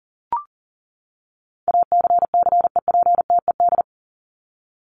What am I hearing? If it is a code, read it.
Morse code, heard as a keyed tone, about 39 wpm, 700 Hz. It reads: ACCEPTED
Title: ED Suffix - Top 100 Words with Base Words 40wpm